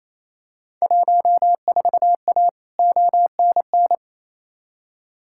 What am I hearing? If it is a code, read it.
Morse code, heard as a keyed tone, about 28 wpm, 700 Hz. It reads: 14A ONN
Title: Field Day Exchanges Encore - 28wpm